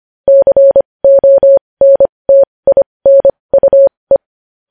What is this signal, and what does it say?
Morse code, heard as a keyed tone, about 25 wpm, spaced as for 20 wpm, 550 Hz. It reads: CONTINUE
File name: podcast / News headlines-1x-25x20